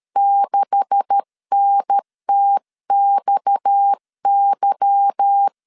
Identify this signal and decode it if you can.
Morse code, heard as a keyed tone, about 13 wpm, 800 Hz. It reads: 6NTXY
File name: LEZIONI / 9 286 12wpm 2305